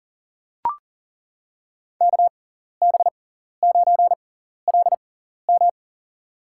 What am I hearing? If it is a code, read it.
Morse code, heard as a keyed tone, about 40 wpm, spaced as for 14 wpm, 700 Hz. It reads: KB9LM
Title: USA Call Signs - 5 Characters - 6x Character Spacing - 40wpm